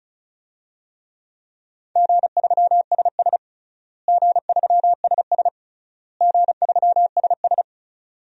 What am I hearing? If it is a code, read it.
Morse code, heard as a keyed tone, about 36 wpm, 700 Hz. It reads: G3SS G3SS G3SS
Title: DXCC-100 Call Signs - 4 Characters - Repeated 3x - 35wpm